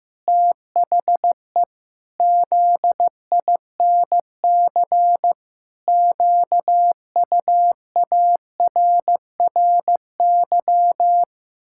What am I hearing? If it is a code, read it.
Morse code, heard as a keyed tone, about 15 wpm, 700 Hz. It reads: THE ZINC QUARRY
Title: Pangram Challenge v2 - 15wpm